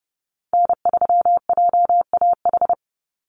Morse code, heard as a keyed tone, about 30 wpm, 700 Hz. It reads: N3JAH
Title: USA Call Signs 30wpm